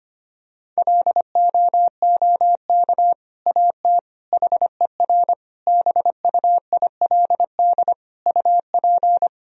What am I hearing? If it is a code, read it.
Morse code, heard as a keyed tone, about 25 wpm, 700 Hz. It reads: LOOK AT HER BUILD UP
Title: Sentences from Top 1000 Words - Review Part 1 - 25wpm